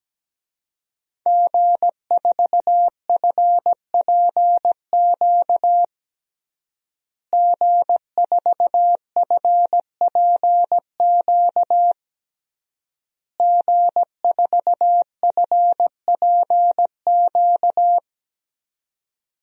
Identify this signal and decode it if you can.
Morse code, heard as a keyed tone, about 17 wpm, 700 Hz. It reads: G4FPQ G4FPQ G4FPQ
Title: DXCC-100 Call Signs - 5 Characters - Repeated 3x - 17wpm